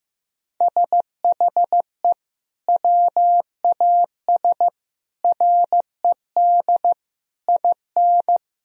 Morse code, heard as a keyed tone, about 16 wpm, 700 Hz. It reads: SHE WAS RED IN
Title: Sentences from Top 1000 Words - Review Part 2 - 15wpm